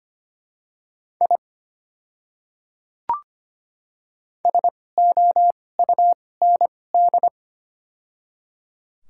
Morse code, heard as a keyed tone, about 25 wpm, spaced as for 18 wpm, 700 Hz. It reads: I SOUND
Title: 2x Character Spacing - Top 100 Words - 25wpm